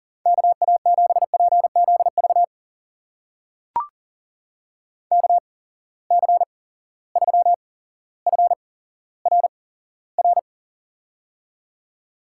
Morse code, heard as a keyed tone, about 42 wpm, spaced as for 17 wpm, 700 Hz. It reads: KA7PZV KC3FRR
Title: USA Call Signs - 6 Characters - 8x Character Spacing - 40wpm